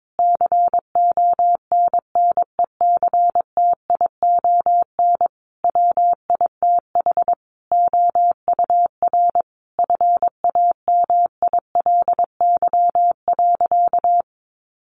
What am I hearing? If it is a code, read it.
Morse code, heard as a keyed tone, about 22 wpm, 700 Hz. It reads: CONNECTION WITH OUR FAMILY.
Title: Sentences from Top 1300 Words 22wpm